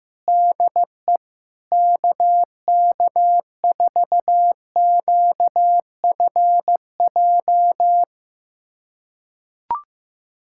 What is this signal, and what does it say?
Morse code, heard as a keyed tone, about 15 wpm, 700 Hz. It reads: DE KK4QFJ
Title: USA Callsign DE Callsign - 6 Characters - 15wpm